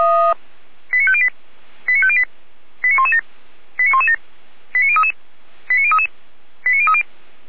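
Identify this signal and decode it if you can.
Five-tone selective call (ZVEI), heard as quick runs of five tones, tones of about 70 ms. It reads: 89508 89508 89117 89117 89033 89033 89030
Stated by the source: Five-tone ZVEI selective call, received by SDR.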